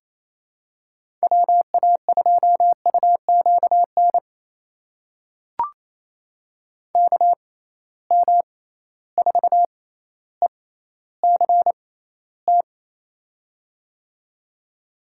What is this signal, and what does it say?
Morse code, heard as a keyed tone, about 28 wpm, spaced as for 14 wpm, 700 Hz. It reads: WA2UQN KM4ECT
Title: USA Call Signs - 6 Characters - 6x Character Spacing - 28wpm